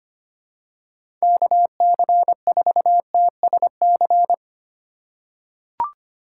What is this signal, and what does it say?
Morse code, heard as a keyed tone, about 25 wpm, 700 Hz. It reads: KC4TSC
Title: USA Call Signs - 6 Characters - 3x Character Spacing - 25wpm